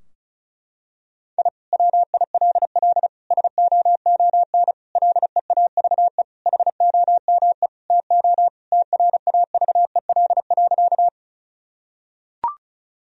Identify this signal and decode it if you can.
Morse code, heard as a keyed tone, about 36 wpm, 700 Hz. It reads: I WILL SOON LEAVE HOME TO TRAVEL.
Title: Sentences from Top 1000 Words - Review Part 2 - 35wpm